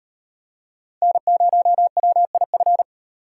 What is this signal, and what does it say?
Morse code, heard as a keyed tone, about 36 wpm, 700 Hz. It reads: N0WIF